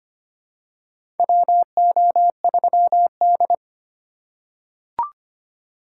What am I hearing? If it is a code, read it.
Morse code, heard as a keyed tone, about 25 wpm, 700 Hz. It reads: WO3D
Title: USA Call Signs - 4 Characters - 2x Character Spacing - 25wpm